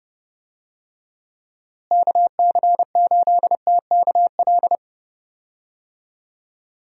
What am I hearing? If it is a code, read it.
Morse code, heard as a keyed tone, about 31 wpm, 700 Hz. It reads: KC8TKL
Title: USA Call Signs - 6 Characters - Repeated 3x - 30wpm